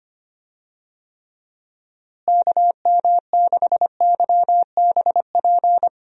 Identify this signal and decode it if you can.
Morse code, heard as a keyed tone, about 25 wpm, 700 Hz. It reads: KM6YBP
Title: USA Call Signs - 6 Characters - Repeated 3x - 25wpm